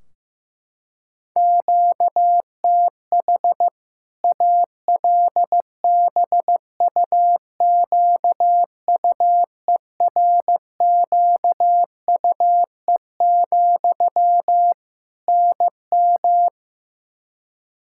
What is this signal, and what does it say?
Morse code, heard as a keyed tone, about 16 wpm, 700 Hz. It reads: QTH ALBUQUERQUE, NM
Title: Practice QSO - Exchange 1 - Beginning - 15wpm